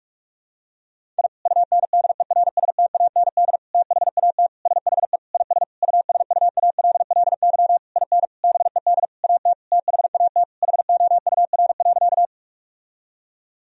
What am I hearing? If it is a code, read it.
Morse code, heard as a keyed tone, about 46 wpm, 700 Hz. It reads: I UNDERSTAND THAT SHE IS USUALLY IN BED AT THAT HOUR.
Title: Sentences from Top 1000 Words - Review Part 3 - 45wpm